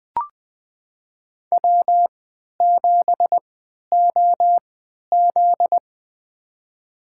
Morse code, heard as a keyed tone, about 20 wpm, spaced as for 11 wpm, 700 Hz. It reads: W7OZ
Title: USA Call Signs - 4 Characters - 3x Character Spacing - 20wpm